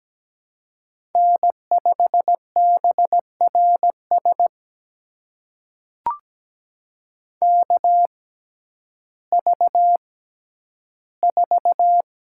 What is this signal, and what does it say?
Morse code, heard as a keyed tone, about 17 wpm, spaced as for 10 wpm, 700 Hz. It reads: N5BRS KV4
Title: USA Call Signs - 5 Characters - 6x Character Spacing - 17wpm